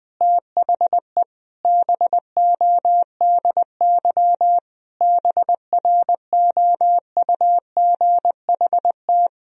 Morse code, heard as a keyed tone, about 20 wpm, 700 Hz. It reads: THE BODY BROUGHT
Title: Sentences from Top 1000 Words - Review Part 2 - 20wpm